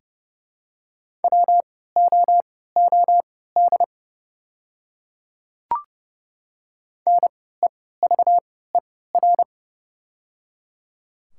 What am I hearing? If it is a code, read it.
Morse code, heard as a keyed tone, about 30 wpm, spaced as for 17 wpm, 700 Hz. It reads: WOOD NEVER
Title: 3x Character Spacing - Top 500 Words - 30wpm